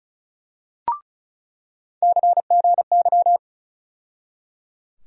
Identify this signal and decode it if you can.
Morse code, heard as a keyed tone, about 34 wpm, 700 Hz. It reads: CGY
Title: USA Call Sign Postfixes - 3 Letters 35wpm